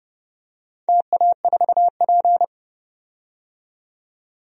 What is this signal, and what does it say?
Morse code, heard as a keyed tone, about 32 wpm, 700 Hz. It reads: TA4P
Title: DXCC-100 Call Signs - 4 Characters - 30wpm